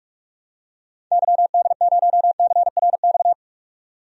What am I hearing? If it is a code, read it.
Morse code, heard as a keyed tone, about 41 wpm, 700 Hz. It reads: YD0CRX
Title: DXCC-100 Call Signs - 6 Characters - 45wpm